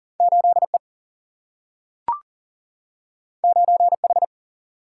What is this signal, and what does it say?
Morse code, heard as a keyed tone, about 40 wpm, 700 Hz. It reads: 8E 9H
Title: Single-Letter Single-Number 40wpm